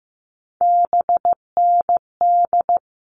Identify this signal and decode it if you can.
Morse code, heard as a keyed tone, about 15 wpm, 700 Hz. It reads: BND
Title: All Cut Numbers - 3-Digit 15wpm